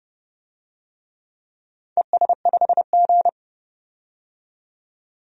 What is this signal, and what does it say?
Morse code, heard as a keyed tone, about 30 wpm, 700 Hz. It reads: ES5G
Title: DXCC-100 Call Signs - 4 Characters - 30wpm